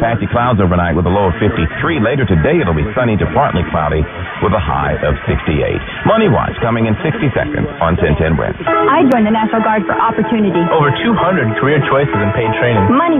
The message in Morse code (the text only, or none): none